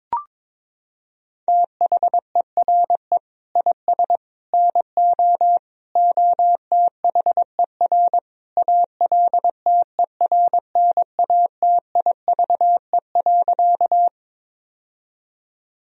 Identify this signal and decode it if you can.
Morse code, heard as a keyed tone, about 22 wpm, 700 Hz. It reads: THERE IS NO OTHER ALTERNATIVE.
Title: Sentences from Top 1000 Words - Review Part 4 - 22wpm